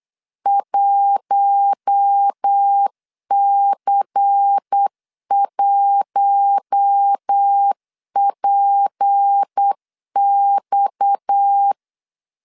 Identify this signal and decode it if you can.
Morse code, heard as a keyed tone, about 8 wpm, 800 Hz. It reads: 1C1PX
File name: CW / DET / 1 6 1500C S A 8wpm 5203